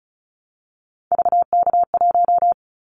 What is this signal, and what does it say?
Morse code, heard as a keyed tone, about 36 wpm, 700 Hz. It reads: VK1
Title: Callsign Prefixes for Top 50 DXCC - 35wpm